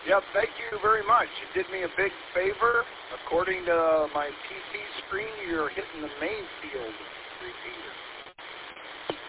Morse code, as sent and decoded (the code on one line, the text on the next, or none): none